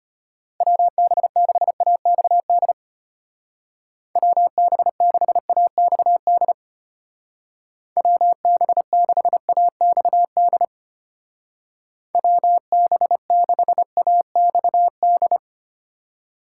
Morse code, heard as a keyed tone, about 30 wpm, 700 Hz. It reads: WB6AXD WB6AXD WB6AXD WB6AXD
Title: Speed Racing - USA Call Signs 25wpm